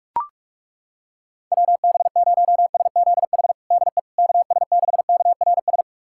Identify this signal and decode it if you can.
Morse code, heard as a keyed tone, about 43 wpm, 700 Hz. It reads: WB0SZH DE KS6KRS
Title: USA Callsign DE Callsign 45wpm